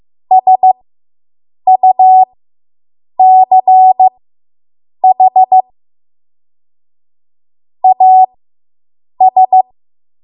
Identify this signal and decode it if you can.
Morse code, heard as a keyed tone, about 15 wpm, spaced as for 7 wpm, 750 Hz. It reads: SUCH AS